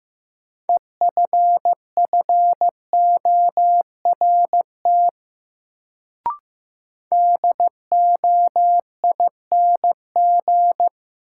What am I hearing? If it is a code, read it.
Morse code, heard as a keyed tone, about 15 wpm, 700 Hz. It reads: EFFORT DOING
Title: Sets of 4 Words - New Word Drill - 15wpm